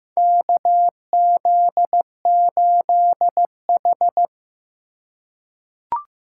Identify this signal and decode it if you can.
Morse code, heard as a keyed tone, about 15 wpm, 700 Hz. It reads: KZ8H
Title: USA Call Signs - 4 Characters - 3x Character Spacing - 15wpm